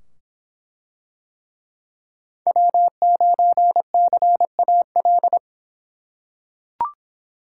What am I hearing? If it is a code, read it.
Morse code, heard as a keyed tone, about 23 wpm, 700 Hz. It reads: W9CAL